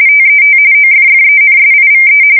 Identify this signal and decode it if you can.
RTTY, about 45.45 baud. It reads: TET OFFENSIVE
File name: fsk170 / USS-New-Jersey